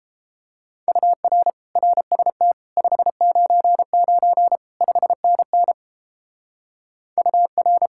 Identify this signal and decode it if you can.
Morse code, heard as a keyed tone, about 29 wpm, 700 Hz. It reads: UR RST 599 5NN UR